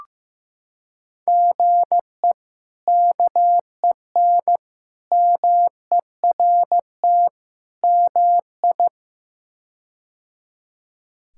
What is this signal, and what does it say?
Morse code, heard as a keyed tone, about 15 wpm, 700 Hz. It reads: GE KEN MERT MI
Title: K1USN SST QSO Practice - 15wpm